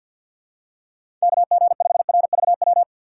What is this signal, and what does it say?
Morse code, heard as a keyed tone, about 52 wpm, 700 Hz. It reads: KG5RVW
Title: USA Call Signs - 4x Character Spacing - 50wpm